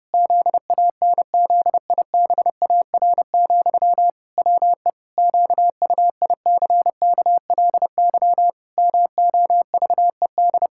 Morse code, heard as a keyed tone, about 30 wpm, 700 Hz. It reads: ZANZIBAR, WE QUICKLY MOVED
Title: Pangram Challenge v2 - 30wpm